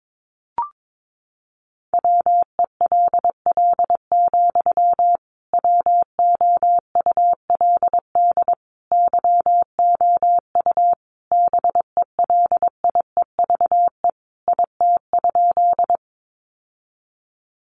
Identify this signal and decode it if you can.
Morse code, heard as a keyed tone, about 22 wpm, 700 Hz. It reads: WELL, WOULD YOU BELIEVE IT?
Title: Sentences from Top 1000 Words 22wpm